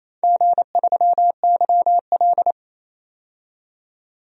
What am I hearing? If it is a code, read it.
Morse code, heard as a keyed tone, about 27 wpm, 700 Hz. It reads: G3YL
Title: DXCC-100 Call Signs - 4 Characters - 28wpm